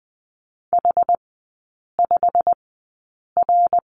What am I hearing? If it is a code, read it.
Morse code, heard as a keyed tone, about 20 wpm, spaced as for 8 wpm, 700 Hz. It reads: H5R
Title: 133 Lesson 22e - Question Mark - Three Characters 20wpm